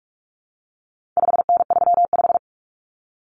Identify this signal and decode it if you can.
Morse code, heard as a keyed tone, about 44 wpm, 700 Hz. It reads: 5N35
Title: Common Cut Numbers - 4-Digit 45wpm